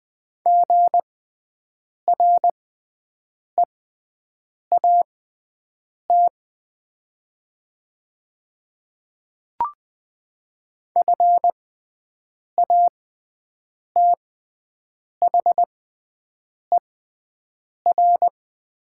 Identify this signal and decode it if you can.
Morse code, heard as a keyed tone, about 20 wpm, spaced as for 7 wpm, 700 Hz. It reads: GREAT FATHER
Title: 6x Character Spacing - Top 200 Words - 20wpm